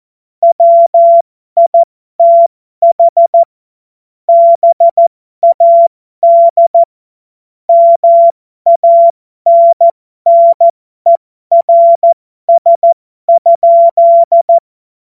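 Morse code, heard as a keyed tone, about 14 wpm, spaced as for 12 wpm, 680 Hz. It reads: WITH BAD MANNERS?
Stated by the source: Amateur radio CW practice text made of jokes.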